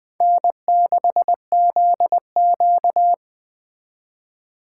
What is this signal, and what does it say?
Morse code, heard as a keyed tone, about 20 wpm, 700 Hz. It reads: N6ZQ